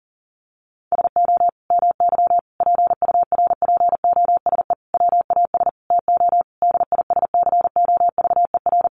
Morse code, heard as a keyed tone, about 40 wpm, 700 Hz. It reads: SO MY PURPOSE WAS TO DISCOVER
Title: Sentences from Top 1100 Words 40wpm